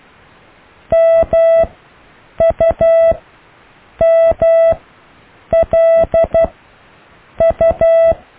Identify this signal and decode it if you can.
Morse code, heard as a keyed tone, about 12 wpm, 660 Hz. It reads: M U M L U
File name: cw / IJLMU